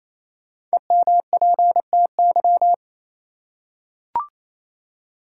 Morse code, heard as a keyed tone, about 28 wpm, 700 Hz. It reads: EMPTY
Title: Five Letter Words - 28wpm